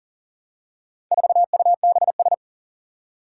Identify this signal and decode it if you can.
Morse code, heard as a keyed tone, about 40 wpm, 700 Hz. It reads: 4UBS
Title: Four Challenging Letter-Number Combinations 40wpm